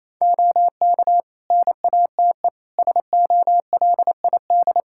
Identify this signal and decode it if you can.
Morse code, heard as a keyed tone, about 28 wpm, 700 Hz. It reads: OK NATE SOLID
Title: Practice QSO - Exchange 3 - More Info - 28wpm